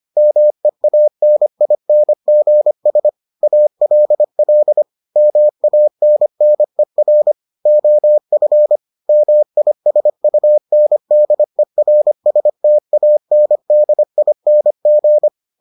Morse code, heard as a keyed tone, about 25 wpm, 600 Hz. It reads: MEANINGS ALL MANNER OF MISUNDERSTANDING